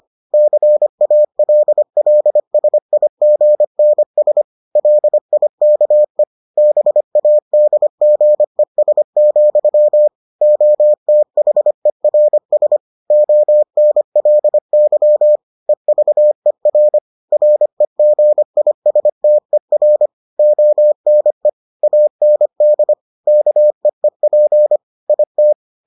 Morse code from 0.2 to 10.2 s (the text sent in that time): CALLSIGNS LIKE BADGES,